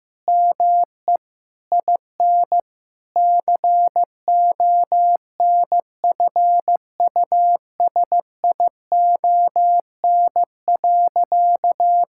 Morse code, heard as a keyed tone, about 16 wpm, 700 Hz. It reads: ME IN CONFUSION.
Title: Sentences from Top 1600 Words 15wpm